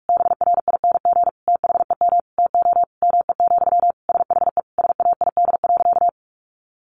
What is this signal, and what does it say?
Morse code, heard as a keyed tone, about 44 wpm, 700 Hz. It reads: BRING THEM TO ME, SHE SAID.
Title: Sentences from Top 500 Words 45wpm